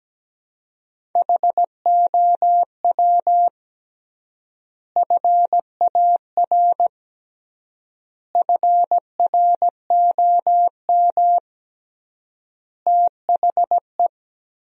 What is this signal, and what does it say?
Morse code, heard as a keyed tone, about 17 wpm, 700 Hz. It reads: HOW FAR FROM THE
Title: 3x Word-Spacing - Sentences from Top 400 Words - 17wpm